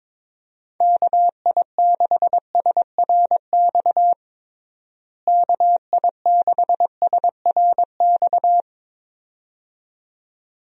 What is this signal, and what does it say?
Morse code, heard as a keyed tone, about 22 wpm, 700 Hz. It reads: KI6SRX KI6SRX
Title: USA Call Signs - 6 Characters - Repeated 2x - 22wpm